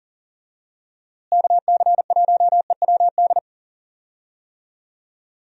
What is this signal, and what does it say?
Morse code, heard as a keyed tone, about 40 wpm, 700 Hz. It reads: KC1EWD
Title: USA Call Signs - 6 Characters - Repeated 2x - 40wpm